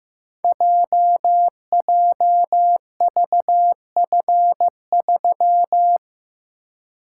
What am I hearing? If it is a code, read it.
Morse code, heard as a keyed tone, about 15 wpm, 700 Hz. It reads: JJVF3
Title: Code Groups - 5 Letter:Numbers - 15wpm